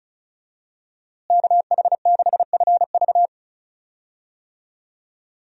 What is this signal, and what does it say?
Morse code, heard as a keyed tone, about 36 wpm, 700 Hz. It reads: KH6FV
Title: USA Call Signs - 5 Characters - Repeated 2x - 35wpm